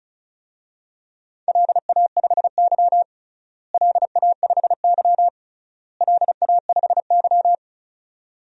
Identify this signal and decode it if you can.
Morse code, heard as a keyed tone, about 34 wpm, 700 Hz. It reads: LA5Y LA5Y LA5Y
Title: DXCC-100 Call Signs - 4 Characters - Repeated 3x - 35wpm